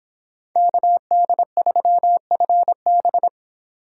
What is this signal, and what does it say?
Morse code, heard as a keyed tone, about 25 wpm, 700 Hz. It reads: KD3FB